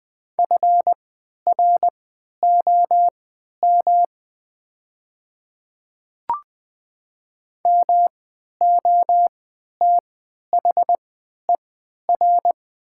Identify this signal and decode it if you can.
Morse code, heard as a keyed tone, about 20 wpm, spaced as for 11 wpm, 700 Hz. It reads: FROM MOTHER
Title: 3x Character Spacing - Top 500 Words - 20wpm